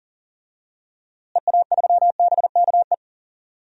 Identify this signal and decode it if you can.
Morse code, heard as a keyed tone, about 37 wpm, 700 Hz. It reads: EA3BKE
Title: DXCC-100 Call Signs - 6 Characters - Repeated 3x - 40wpm